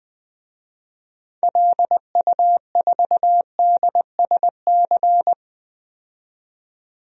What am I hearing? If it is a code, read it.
Morse code, heard as a keyed tone, about 21 wpm, 700 Hz. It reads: LU4DSC
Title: DXCC-100 Call Signs - 6 Characters - 20wpm